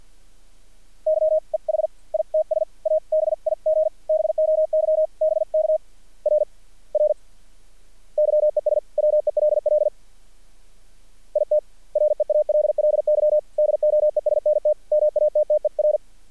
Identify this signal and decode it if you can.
Morse code, heard as a keyed tone, about 48 wpm, 600 Hz. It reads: YEH ITS A BIG BOY BK RR YES WELL IT REALLY DOESNT MATTER
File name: archived / CRSnet